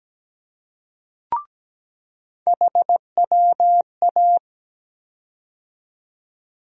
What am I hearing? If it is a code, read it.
Morse code, heard as a keyed tone, about 17 wpm, 700 Hz. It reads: HWA
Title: USA Call Sign Postfixes - 3 Letters 17wpm